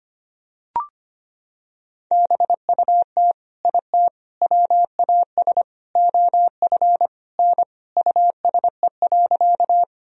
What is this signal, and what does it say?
Morse code, heard as a keyed tone, about 25 wpm, 700 Hz. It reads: BUT IT WAS OF N USE.
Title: Sentences from Top 100 Words and Abbreviations 25wpm